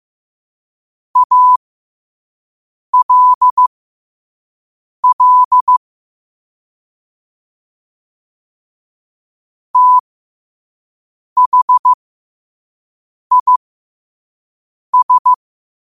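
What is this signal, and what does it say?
Morse code, heard as a keyed tone, about 15 wpm, spaced as for 5 wpm, 1000 Hz. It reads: ALL THIS
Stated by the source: Morse code sent at 5 wpm.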